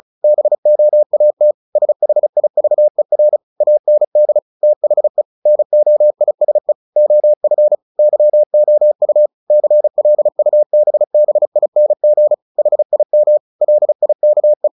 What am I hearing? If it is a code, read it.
Morse code, heard as a keyed tone, about 34 wpm, 600 Hz. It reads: BOAT SHIVER AND THE NOISE OF YOU CLUBBING HIM LIKE
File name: the old man and the sea / the old man and the sea 1 35wpm